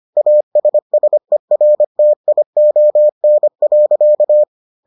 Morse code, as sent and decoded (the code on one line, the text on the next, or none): .- ... ... .-. - .. --- -. .-.-.-
ASSERTION.